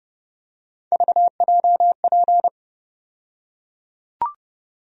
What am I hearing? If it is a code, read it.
Morse code, heard as a keyed tone, about 30 wpm, 700 Hz. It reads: VJP
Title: USA Call Sign Postfixes - 3 Letters 30wpm